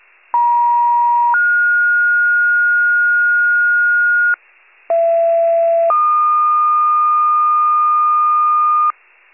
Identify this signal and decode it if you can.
Two-tone page, heard as a one-second tone, then a three-second tone, about 950 then 1430 Hz, about 670 then 1220 Hz, with noise. It is heard from 0.3 to 8.9 s.